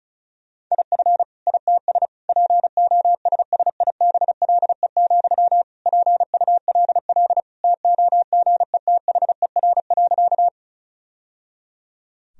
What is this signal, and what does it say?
Morse code, heard as a keyed tone, about 34 wpm, 700 Hz. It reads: IF ITS POSSIBLE, PULL TOGETHER.
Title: Sentences from Top 1000 Words - Review Part 2 - 35wpm